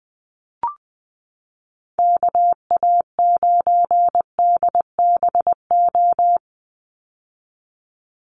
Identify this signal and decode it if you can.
Morse code, heard as a keyed tone, about 20 wpm, 700 Hz. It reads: KA9DBO
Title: USA Call Signs 20wpm